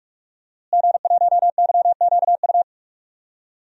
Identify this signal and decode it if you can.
Morse code, heard as a keyed tone, about 42 wpm, 700 Hz. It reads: G1YQU
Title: DXCC-100 Call Signs - 5 Characters - Repeated 2x - 45wpm